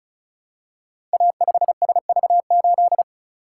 Morse code, heard as a keyed tone, about 36 wpm, 690 Hz. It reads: A5SV8
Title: Code Groups - 5 Letter:Numbers - 35wpm